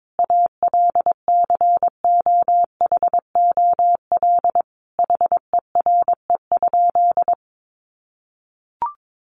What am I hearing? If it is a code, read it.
Morse code, heard as a keyed tone, about 22 wpm, 700 Hz. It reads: ALCOHOL HERE?
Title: Sentences from Top 1400 Words 22wpm